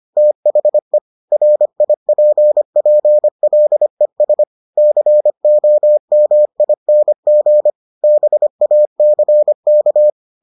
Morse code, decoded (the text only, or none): THE RIPPLES COMING BACK